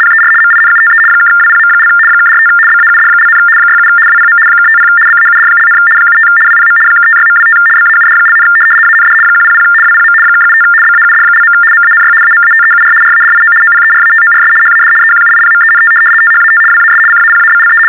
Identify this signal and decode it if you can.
RTTY, about 75 baud. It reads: PER MINUTE (WPM) (45 BAUD), 67 WPM (50 BAUD), 75 WPM (56.25 BAUD), AND 100 WPM (75 BAUD). MANY AMATEUR RADIO OPERATORS HAD EQUIPMENT THAT WAS CAPABLE
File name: teletype-RTTY-75